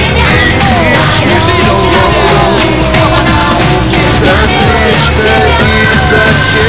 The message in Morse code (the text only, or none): none